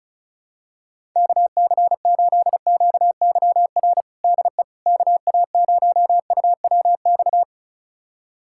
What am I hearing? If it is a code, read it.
Morse code, heard as a keyed tone, about 36 wpm, 700 Hz. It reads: KC8QYR DE KA0UWX
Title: USA Callsign DE Callsign - 6 Characters - 35wpm